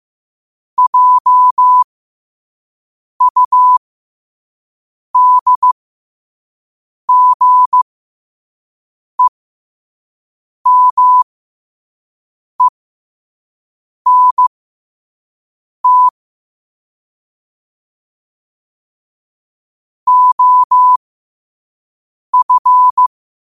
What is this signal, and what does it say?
Morse code, heard as a keyed tone, about 15 wpm, spaced as for 5 wpm, 1000 Hz. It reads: JUDGEMENT OF
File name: 5wpm